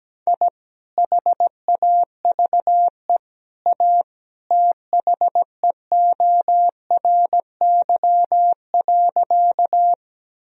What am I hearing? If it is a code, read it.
Morse code, heard as a keyed tone, about 17 wpm, 700 Hz. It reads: I HAVE A THEORY.
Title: Sentences from Top 1000 Words - Review Part 3 - 17wpm